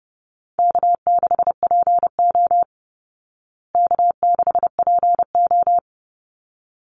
Morse code, heard as a keyed tone, about 30 wpm, 700 Hz. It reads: K6PO K6PO
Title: Call Signs Repeated 3x - 30wpm